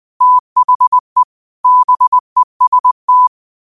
Morse code, heard as a keyed tone, about 20 wpm, 1000 Hz. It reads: THE BEST